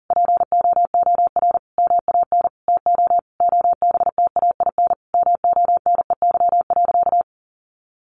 Morse code, heard as a keyed tone, about 40 wpm, 700 Hz. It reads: POOR MAN TO OBTAIN MONEY.
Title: Sentences from Top 1300 Words 40wpm